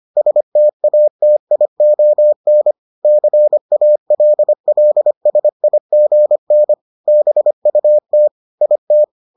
Morse code, heard as a keyed tone, about 25 wpm, 600 Hz. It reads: STATION CALLSIGN BUT IT